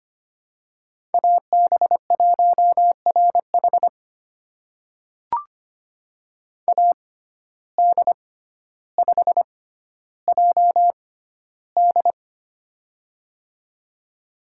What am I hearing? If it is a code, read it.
Morse code, heard as a keyed tone, about 25 wpm, spaced as for 13 wpm, 700 Hz. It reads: AB1RH AD5JD
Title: USA Call Signs - 5 Characters - 6x Character Spacing - 25wpm